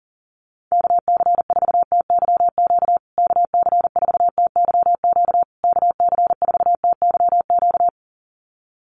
Morse code, heard as a keyed tone, about 40 wpm, 700 Hz. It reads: KC4TYQ KC4TYQ KC4TYQ
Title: USA Call Signs Repeated 3x 40wpm